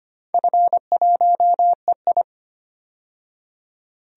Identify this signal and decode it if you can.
Morse code, heard as a keyed tone, about 25 wpm, 700 Hz. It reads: F1EI
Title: DXCC-100 Call Signs - 4 Characters - 25wpm